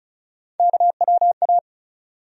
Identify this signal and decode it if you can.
Morse code, heard as a keyed tone, about 36 wpm, 700 Hz. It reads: KWA
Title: USA Call Sign Postfixes - 3 Letters 35wpm